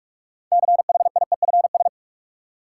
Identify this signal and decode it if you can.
Morse code, heard as a keyed tone, about 43 wpm, 700 Hz. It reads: CHIEFS
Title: Sets of 5 Words Encore - New Word Drill - 45wpm